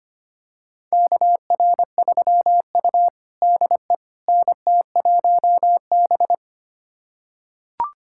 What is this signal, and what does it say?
Morse code, heard as a keyed tone, about 25 wpm, 700 Hz. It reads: KR3U DE NT1B
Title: USA Callsign DE Callsign - 4 Characters - 25wpm